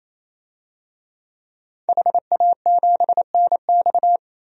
Morse code, heard as a keyed tone, about 28 wpm, 700 Hz. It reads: HA7NX
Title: DXCC-100 Call Signs - 5 Characters - 28wpm